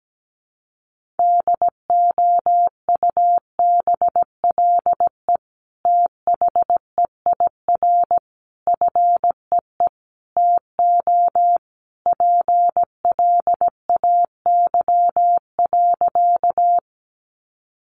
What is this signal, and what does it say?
Morse code, heard as a keyed tone, about 17 wpm, 700 Hz. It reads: DOUBLE THEIR FEE TO PLAY.
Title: Sentences from Top 1500 Words 17wpm